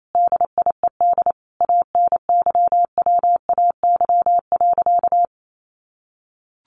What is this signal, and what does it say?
Morse code, heard as a keyed tone, about 28 wpm, 700 Hz. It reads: DIED ANYWAY.
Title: A Sound Of Thunder 28wpm